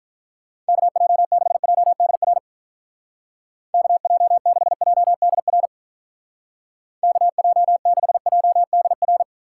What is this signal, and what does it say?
Morse code, heard as a keyed tone, about 42 wpm, 700 Hz. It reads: KJ6JDR KJ6JDR KJ6JDR
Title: Speed Racing - USA 2x3 Call Signs 35wpm